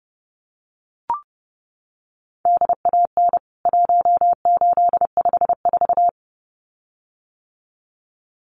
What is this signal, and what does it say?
Morse code, heard as a keyed tone, about 30 wpm, 700 Hz. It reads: DAN 1854
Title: CWT Exchange 30wpm